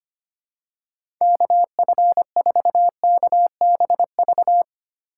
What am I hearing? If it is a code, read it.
Morse code, heard as a keyed tone, about 25 wpm, 700 Hz. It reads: KF4KBV
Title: USA Call Signs - 8x Character Spacing - 25wpm